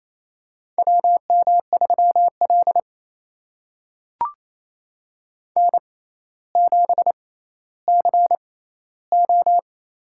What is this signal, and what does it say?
Morse code, heard as a keyed tone, about 28 wpm, spaced as for 14 wpm, 700 Hz. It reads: WM3L N7CO
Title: USA Call Signs - 4 Characters - 6x Character Spacing - 28wpm